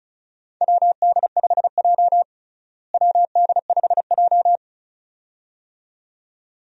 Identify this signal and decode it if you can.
Morse code, heard as a keyed tone, about 34 wpm, 700 Hz. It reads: WD5J WD5J
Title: USA Call Signs - 4 Characters - Repeated 2x - 35wpm